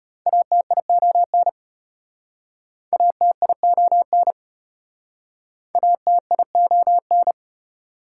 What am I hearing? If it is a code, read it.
Morse code, heard as a keyed tone, about 34 wpm, 700 Hz. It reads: ATION ATION ATION